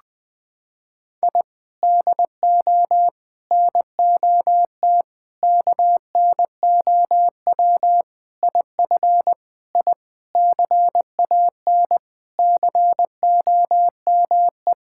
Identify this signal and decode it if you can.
Morse code, heard as a keyed tone, about 21 wpm, 700 Hz. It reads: I DO NOT KNOW IF I CAN COME
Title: Sentences with 5-Word Phrases - 20wpm